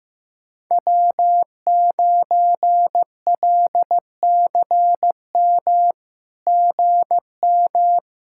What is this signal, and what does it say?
Morse code, heard as a keyed tone, about 15 wpm, 700 Hz. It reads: W9LCM GM
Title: SOTA QSO Practice 15wpm